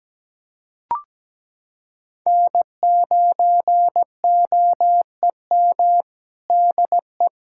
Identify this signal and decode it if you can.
Morse code, heard as a keyed tone, about 17 wpm, 700 Hz. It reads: N9OEM DE
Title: USA Callsign DE Callsign - 5 Characters - 17wpm